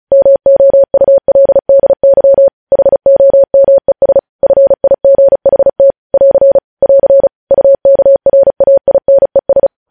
Morse code, heard as a keyed tone, about 36 wpm, 550 Hz. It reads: MOULDY HOMES FIGHT <AR> <AR> UKRAINES
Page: news headlines in morse code at 35 wpm